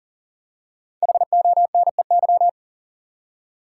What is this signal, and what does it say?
Morse code, heard as a keyed tone, about 40 wpm, 700 Hz. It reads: HONEY
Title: Five Letter Words - 40wpm